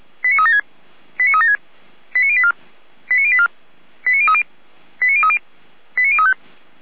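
Five-tone selective call (ZVEI), heard as quick runs of five tones, tones of about 80 ms. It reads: 89376 89376 89084 89084 89030 89030 89035
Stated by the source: Received on SDR as ZVEI five-tone selective call.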